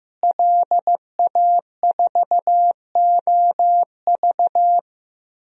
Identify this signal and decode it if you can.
Morse code, heard as a keyed tone, about 16 wpm, 700 Hz. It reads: LA4OV
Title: DXCC-100 Call Signs - 5 Characters - Repeated 3x - 15wpm